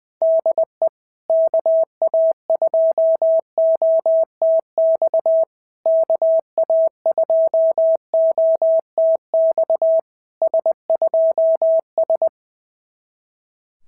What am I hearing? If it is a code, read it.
Morse code, heard as a keyed tone, about 21 wpm, 650 Hz. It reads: DE KA2OTX KA2OTX S2S
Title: SOTA QSO Practice 20wpm